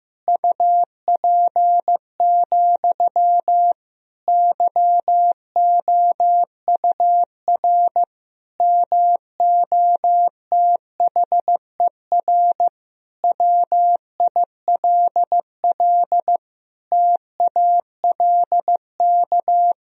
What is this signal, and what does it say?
Morse code, heard as a keyed tone, about 15 wpm, 700 Hz. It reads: UP, YOUR MOTHER WILL TALK
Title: Sentences from Top 1000 Words - Review Part 1 - 15wpm